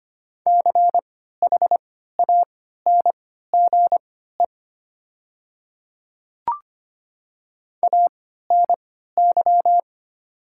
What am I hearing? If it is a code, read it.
Morse code, heard as a keyed tone, about 25 wpm, spaced as for 14 wpm, 700 Hz. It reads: CHANGE ANY
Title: 3x Character Spacing - Top 200 Words - 25wpm